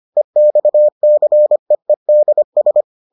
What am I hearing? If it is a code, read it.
Morse code, heard as a keyed tone, about 25 wpm, 600 Hz. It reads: EXCEEDS